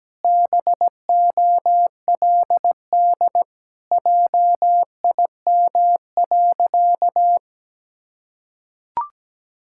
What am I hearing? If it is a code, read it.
Morse code, heard as a keyed tone, about 17 wpm, 700 Hz. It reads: BOLD JIM.
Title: Pangram Challenge v2 - 17wpm